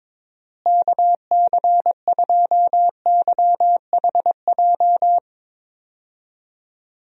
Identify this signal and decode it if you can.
Morse code, heard as a keyed tone, about 21 wpm, 700 Hz. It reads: KC2YHJ
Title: USA Call Signs - 6 Characters - Repeated 3x - 22wpm